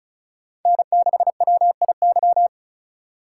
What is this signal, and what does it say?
Morse code, heard as a keyed tone, about 36 wpm, 700 Hz. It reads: N6WIY
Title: USA Call Signs - 5 Characters - 4x Character Spacing - 35wpm